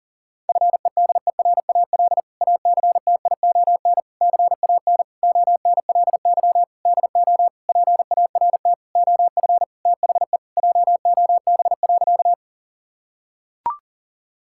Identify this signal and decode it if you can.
Morse code, heard as a keyed tone, about 37 wpm, 700 Hz. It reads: FEDERAL ACTION CAN ONLY DO PART OF THE JOB.
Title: Sentences from Top 1000 Words - Review Part 5 - 40wpm